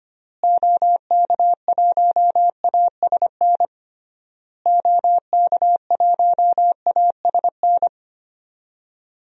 Morse code, heard as a keyed tone, about 25 wpm, 700 Hz. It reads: OK1ASN OK1ASN
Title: DXCC-100 Call Signs - 6 Characters - Repeated 3x - 25wpm